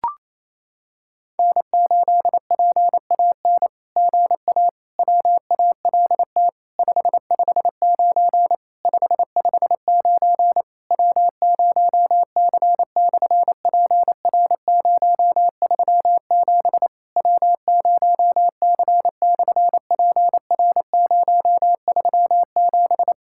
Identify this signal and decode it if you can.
Morse code, heard as a keyed tone, about 28 wpm, 700 Hz. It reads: N8PAN GA WALT 559 559 W0C/PR037 W0C/PR037
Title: SOTA QSO Practice 28wpm